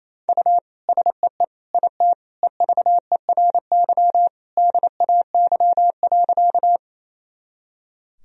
Morse code, heard as a keyed tone, about 28 wpm, 700 Hz. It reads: U SEE IT EVERY DAY.
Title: Sentences from Top 200 Words and Abbreviations 28wpm